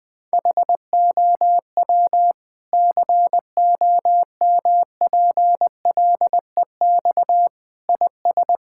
Morse code, heard as a keyed tone, about 20 wpm, 700 Hz. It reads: HOW COMPLEX IS
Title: Sentences from Top 1000 Words - Review Part 6 - 20wpm